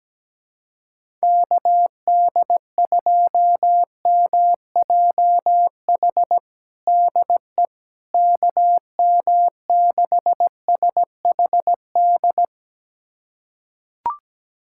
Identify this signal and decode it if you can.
Morse code, heard as a keyed tone, about 17 wpm, 700 Hz. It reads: KD2MJH DE KM6SHD
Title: USA Callsign DE Callsign 17wpm